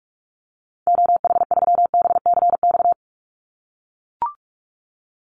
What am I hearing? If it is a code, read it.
Morse code, heard as a keyed tone, about 44 wpm, 700 Hz. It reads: OH3BCX
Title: Call Signs - 45wpm